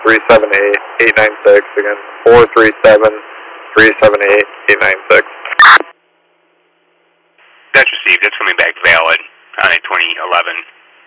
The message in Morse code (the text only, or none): none